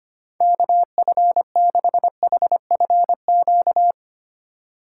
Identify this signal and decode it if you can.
Morse code, heard as a keyed tone, about 25 wpm, 700 Hz. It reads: KF6HFQ
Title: USA Call Signs - 3x Character Spacing - 25wpm